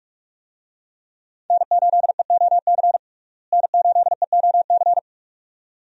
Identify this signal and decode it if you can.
Morse code, heard as a keyed tone, about 41 wpm, 700 Hz. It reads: N8EOC N8EOC
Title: USA Call Signs - 5 Characters - Repeated 2x - 45wpm